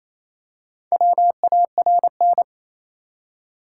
Morse code, heard as a keyed tone, about 28 wpm, 700 Hz. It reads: WARN
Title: Four Letter Words - 28wpm